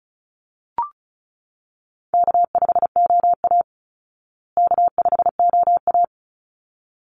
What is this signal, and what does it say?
Morse code, heard as a keyed tone, about 34 wpm, 700 Hz. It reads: K5OA K5OA
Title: Easy Call Signs Repeated 2x - 35wpm